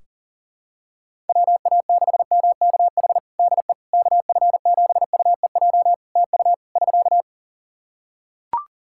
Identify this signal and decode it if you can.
Morse code, heard as a keyed tone, about 40 wpm, 700 Hz. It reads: WA6MKH DE KF7UEJ TU <SK>
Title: K1USN SST QSO Practice 40wpm